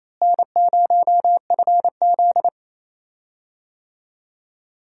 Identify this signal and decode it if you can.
Morse code, heard as a keyed tone, about 28 wpm, 700 Hz. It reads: N0FZ
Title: USA Call Signs - 4 Characters - Repeated 2x - 28wpm